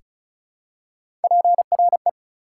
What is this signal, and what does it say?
Morse code, heard as a keyed tone, about 34 wpm, 700 Hz. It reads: PRE
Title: Prefixes and Suffixes 35wpm